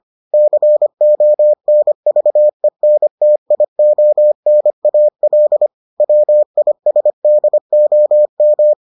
Morse code, heard as a keyed tone, about 25 wpm, 600 Hz. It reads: CONVENTIONAL WISDOM